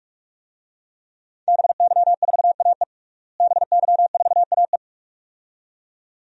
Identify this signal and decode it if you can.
Morse code, heard as a keyed tone, about 46 wpm, 700 Hz. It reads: BY4AE BY4AE
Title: DXCC-100 Call Signs - 5 Characters - Repeated 2x - 45wpm